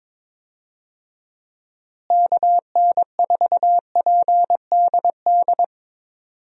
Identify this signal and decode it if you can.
Morse code, heard as a keyed tone, about 22 wpm, 700 Hz. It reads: KN4PDD
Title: USA Call Signs - 6 Characters - Repeated 2x - 22wpm